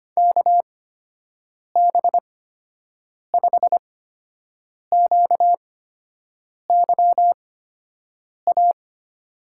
Morse code, heard as a keyed tone, about 25 wpm, spaced as for 7 wpm, 700 Hz. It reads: KB5QYA
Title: USA Call Signs - 6 Characters - 8x Character Spacing - 25wpm